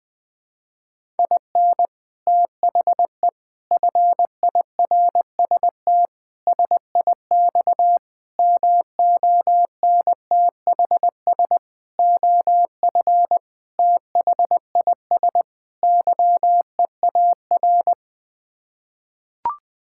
Morse code, heard as a keyed tone, about 20 wpm, 700 Hz. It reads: IN THE FIRST SIX MONTHS OF THIS YEAR
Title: Sets of 8 Words - 20wpm